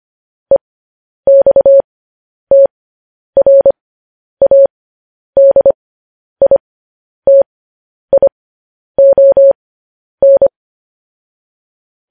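Morse code, heard as a keyed tone, about 25 wpm, spaced as for 10 wpm, 550 Hz. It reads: EXTRADITION
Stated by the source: Morse code news headlines practice transmission.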